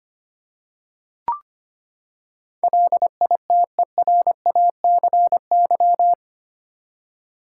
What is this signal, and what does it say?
Morse code, heard as a keyed tone, about 25 wpm, 700 Hz. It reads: LITERACY
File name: Eight Letter Words - 25wpm